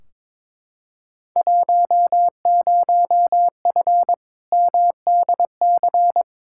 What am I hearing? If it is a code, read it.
Morse code, heard as a keyed tone, about 22 wpm, 700 Hz. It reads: 10F MDC
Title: Field Day Exchanges Encore - 22wpm